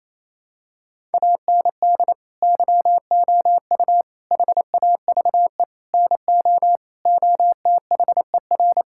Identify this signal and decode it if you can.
Morse code, heard as a keyed tone, about 28 wpm, 700 Hz. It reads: AND YOU HAVE NO OTHER
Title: Sentences from Top 1000 Words - Review Part 6 - 28wpm